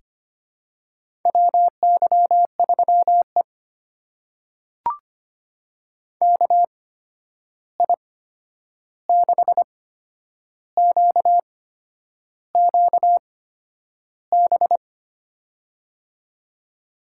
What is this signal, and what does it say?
Morse code, heard as a keyed tone, about 25 wpm, spaced as for 9 wpm, 700 Hz. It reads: WY3E KI6QQB
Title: USA Call Signs - 8x Character Spacing - 25wpm